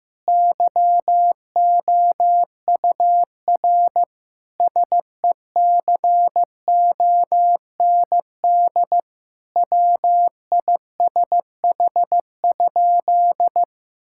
Morse code, heard as a keyed tone, about 15 wpm, 700 Hz. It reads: YOUR SECOND WISH?
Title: Sentences from Top 1600 Words 15wpm